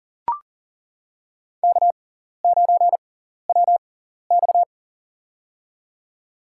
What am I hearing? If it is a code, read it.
Morse code, heard as a keyed tone, about 40 wpm, spaced as for 14 wpm, 700 Hz. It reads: K9WX
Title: USA Call Signs - 6x Character Spacing - 40wpm